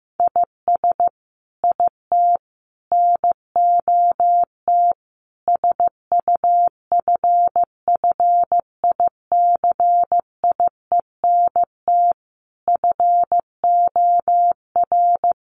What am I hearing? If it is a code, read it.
Morse code, heard as a keyed tone, about 15 wpm, 700 Hz. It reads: IS IT NOT SUFFICIENT FOR
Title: Sentences from Top 1400 Words 15wpm